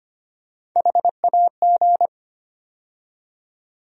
Morse code, heard as a keyed tone, about 25 wpm, 700 Hz. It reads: HAG
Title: USA Call Sign Postfixes - 3 Letters 25wpm